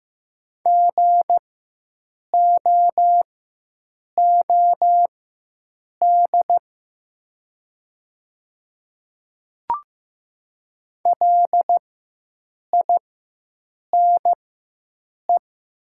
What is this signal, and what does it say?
Morse code, heard as a keyed tone, about 15 wpm, spaced as for 7 wpm, 700 Hz. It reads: GOOD LINE
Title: 4x Character Spacing - Top 200 Words - 15wpm